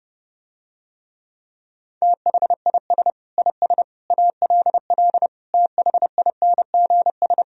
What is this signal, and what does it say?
Morse code, heard as a keyed tone, about 30 wpm, 700 Hz. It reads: THIS IS ALL THINGS